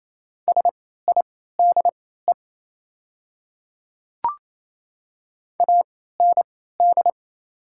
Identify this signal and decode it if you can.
Morse code, heard as a keyed tone, about 28 wpm, spaced as for 16 wpm, 690 Hz. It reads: SIDE AND
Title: 3x Character Spacing - Top 100 Words - 28wpm